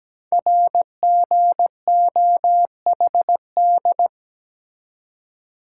Morse code, heard as a keyed tone, about 17 wpm, 700 Hz. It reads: RGOHD
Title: Code Groups - 5 Letters - 17wpm